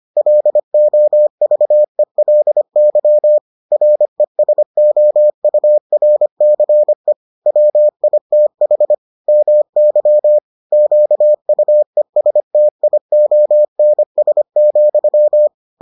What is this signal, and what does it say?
Morse code, heard as a keyed tone, about 25 wpm, 600 Hz. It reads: LOVELY RESOURCE WITH MY QUESTIONS,